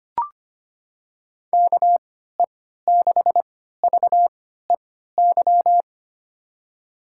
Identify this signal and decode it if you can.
Morse code, heard as a keyed tone, about 25 wpm, spaced as for 14 wpm, 700 Hz. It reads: KE6VEY
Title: USA Call Signs - 6 Characters - 3x Character Spacing - 25wpm